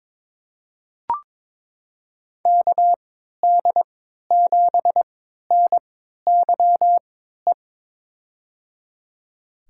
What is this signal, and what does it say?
Morse code, heard as a keyed tone, about 22 wpm, spaced as for 13 wpm, 700 Hz. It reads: KD7NYE
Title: USA Call Signs - 3x Character Spacing - 22wpm